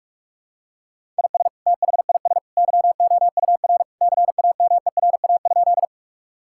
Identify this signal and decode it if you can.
Morse code, heard as a keyed tone, about 46 wpm, 700 Hz. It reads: IS THIS YOUR CAMERA?
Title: Sentences from Top 1000 Words - Review Part 5 - 45wpm